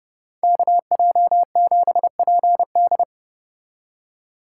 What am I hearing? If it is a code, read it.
Morse code, heard as a keyed tone, about 29 wpm, 700 Hz. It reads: KJ7PD